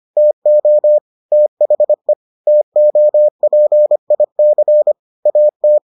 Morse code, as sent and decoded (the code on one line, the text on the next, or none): - --- / - .... / - --- .--. .. -.-. / .- -
TO THE TOPIC AT